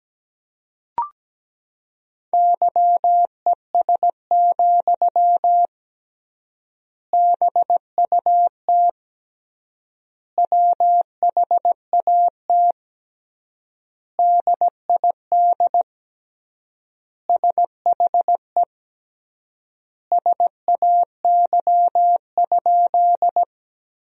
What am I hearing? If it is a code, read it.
Morse code, heard as a keyed tone, about 17 wpm, 700 Hz. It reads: YES, BUT WHAT DID SHE SAY?
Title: 3x Word-Spacing - Sentences from Top 500 Words - 17wpm